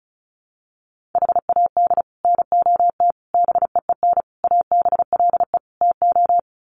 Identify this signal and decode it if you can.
Morse code, heard as a keyed tone, about 34 wpm, 700 Hz. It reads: HAD NOT BEEN ABLE TO
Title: Sets of 5 Words Encore 35wpm